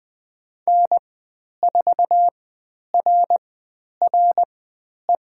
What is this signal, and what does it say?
Morse code, heard as a keyed tone, about 20 wpm, spaced as for 10 wpm, 700 Hz. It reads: N4RRE
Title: 095 Lesson 16g - Letter c - Call Signs 20wpm